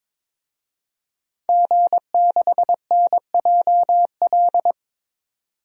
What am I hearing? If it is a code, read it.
Morse code, heard as a keyed tone, about 22 wpm, 700 Hz. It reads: G6NJL
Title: DXCC-100 Call Signs - 5 Characters - 22wpm